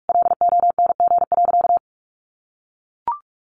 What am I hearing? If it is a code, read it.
Morse code, heard as a keyed tone, about 44 wpm, 700 Hz. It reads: LONG.
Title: Sentences from Top 1300 Words 45wpm